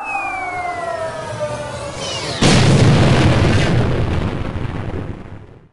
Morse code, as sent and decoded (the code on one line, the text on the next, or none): none